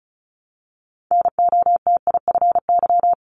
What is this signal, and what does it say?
Morse code, heard as a keyed tone, about 36 wpm, 700 Hz. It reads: NOTIFY
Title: Multi-letter Phonemes Repeated 3x 35wpm